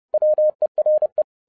Morse code, heard as a keyed tone, about 30 wpm, 610 Hz. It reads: WERE